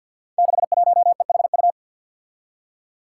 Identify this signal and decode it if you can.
Morse code, heard as a keyed tone, about 52 wpm, 700 Hz. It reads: 61EHU
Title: Five Challenging Letter-Number Combinations 50wpm